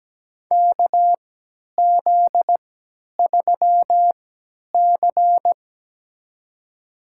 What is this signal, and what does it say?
Morse code, heard as a keyed tone, about 17 wpm, spaced as for 10 wpm, 700 Hz. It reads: KZ3C
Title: USA Call Signs - 4 Characters - 3x Character Spacing - 17wpm